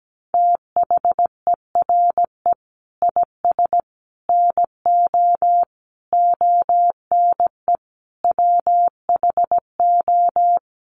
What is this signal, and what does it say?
Morse code, heard as a keyed tone, about 17 wpm, 700 Hz. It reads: THERE IS NO ONE WHO
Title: Sentences from Top 1000 Words 17wpm